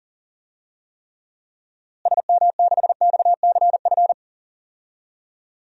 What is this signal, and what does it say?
Morse code, heard as a keyed tone, about 37 wpm, 700 Hz. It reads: SM6XCF
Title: DXCC-100 Call Signs - 6 Characters - Repeated 3x - 40wpm